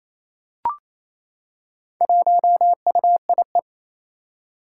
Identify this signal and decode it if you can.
Morse code, heard as a keyed tone, about 28 wpm, 700 Hz. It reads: 1UIE
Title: Four Challenging Letter-Number Combinations 28wpm